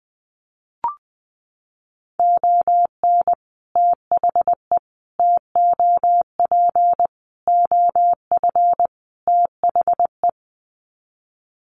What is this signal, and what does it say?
Morse code, heard as a keyed tone, about 20 wpm, 700 Hz. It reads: ON THE TOP OF THE
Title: Sets of 5 Words Encore 20wpm